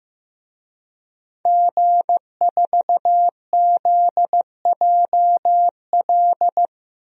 Morse code, heard as a keyed tone, about 15 wpm, 700 Hz. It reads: G4ZJL
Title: DXCC-100 Call Signs - 5 Characters - 15wpm